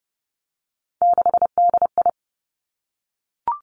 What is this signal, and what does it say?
Morse code, heard as a keyed tone, about 30 wpm, 700 Hz. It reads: 6DI
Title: Three Challenging Letter-Number Combinations 30wpm